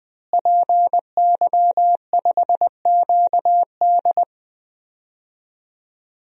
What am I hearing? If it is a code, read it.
Morse code, heard as a keyed tone, about 20 wpm, 700 Hz. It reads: PY5QD
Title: DXCC-100 Call Signs - 5 Characters - Repeated 3x - 20wpm